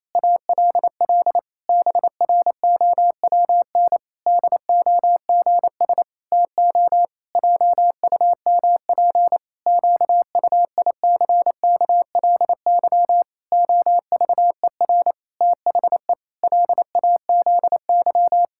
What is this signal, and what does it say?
Morse code, heard as a keyed tone, about 28 wpm, 700 Hz. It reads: ALL BROWN DOGS TO JUMP QUICKLY OVER THE LAZY
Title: Pangram Challenge v2 - 28wpm